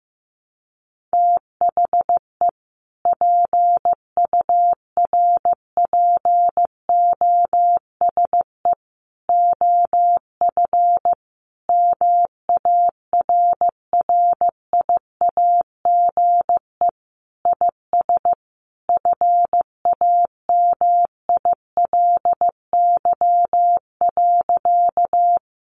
Morse code, heard as a keyed tone, about 15 wpm, 700 Hz. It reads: THE PURPOSE OF MARRIAGE IS FAMILY.
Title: Sentences from Top 800 Words 15wpm